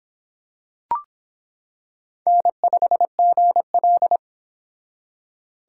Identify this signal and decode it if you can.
Morse code, heard as a keyed tone, about 25 wpm, 700 Hz. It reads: N5GL